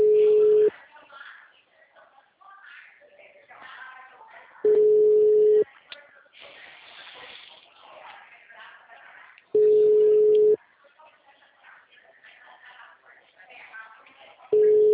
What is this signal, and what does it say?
Ringback tone (Europe, 425 Hz), with speech.